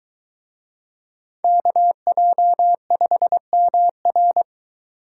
Morse code, heard as a keyed tone, about 22 wpm, 700 Hz. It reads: KJ5MR